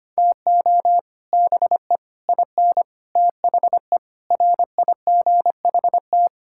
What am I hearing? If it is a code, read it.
Morse code, heard as a keyed tone, about 25 wpm, 700 Hz. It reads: TO BE IN THE RIGHT
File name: Sets of 9 Words - 25wpm